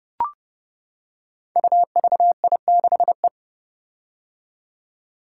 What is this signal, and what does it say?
Morse code, heard as a keyed tone, about 30 wpm, 700 Hz. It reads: UVI6E
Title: Code Groups - 5 Letter:Numbers - 30wpm